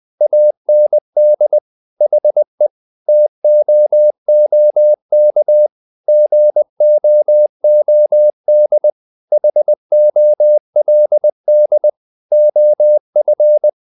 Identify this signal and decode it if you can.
Morse code, heard as a keyed tone, about 20 wpm, 600 Hz. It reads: AND HE TOOK GOOD HOLD OF
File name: the old man and the sea part 2 20wpm